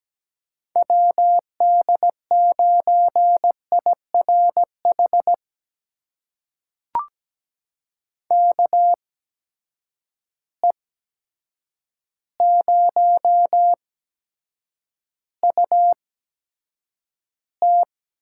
Morse code, heard as a keyed tone, about 17 wpm, spaced as for 8 wpm, 700 Hz. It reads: WD9IRH KE0UT
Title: USA Call Signs - 6 Characters - 8x Character Spacing - 17wpm